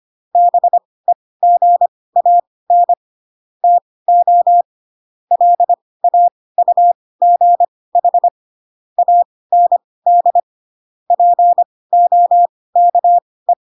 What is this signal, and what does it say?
Morse code, heard as a keyed tone, about 25 wpm, spaced as for 18 wpm, 700 Hz. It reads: BEGAN TO LAUGH AND POKE
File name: Pinocchio ch3 12 25 ik1hge